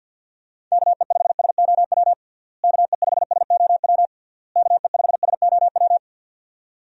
Morse code, heard as a keyed tone, about 51 wpm, 700 Hz. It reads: KE5SOW KE5SOW KE5SOW
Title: USA Call Signs - 6 Characters - Repeated 3x - 50wpm